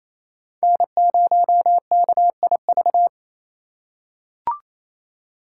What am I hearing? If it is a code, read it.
Morse code, heard as a keyed tone, about 28 wpm, 700 Hz. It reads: N0KIV
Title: USA Call Signs - 5 Characters - 4x Character Spacing - 28wpm